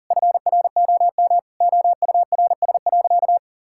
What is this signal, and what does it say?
Morse code, heard as a keyed tone, about 40 wpm, 700 Hz. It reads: FROM OURS.
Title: Sentences from Top 1000 Words - Review Part 5 - 40wpm